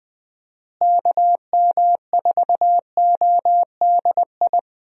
Morse code, heard as a keyed tone, about 20 wpm, 700 Hz. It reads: KM4ODI
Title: USA Call Signs - 6 Characters - Repeated 2x - 20wpm